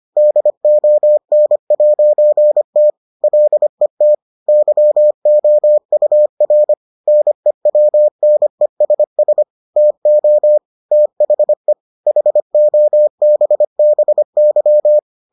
Morse code, heard as a keyed tone, about 25 wpm, 600 Hz. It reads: DON'T LET YOUR NEWNESS TO THE HOBBY